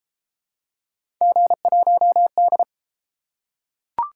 Morse code, heard as a keyed tone, about 31 wpm, 700 Hz. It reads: G1D